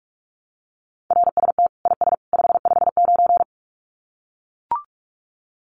Morse code, heard as a keyed tone, about 44 wpm, 700 Hz. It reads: RST IS 559
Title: Common QSO Phrases 45wpm